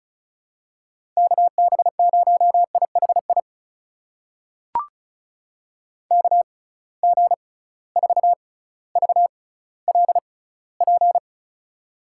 Morse code, heard as a keyed tone, about 34 wpm, spaced as for 18 wpm, 700 Hz. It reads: KB0IHI KG4VLP
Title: USA Call Signs - 6 Characters - 6x Character Spacing - 35wpm